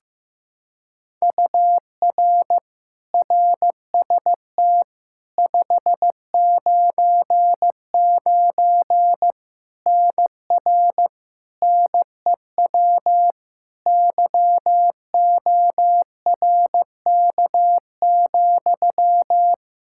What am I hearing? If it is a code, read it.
Morse code, heard as a keyed tone, about 15 wpm, 700 Hz. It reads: UR RST 599 NR NEW YORK,
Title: K1USN SST QSO Practice 15wpm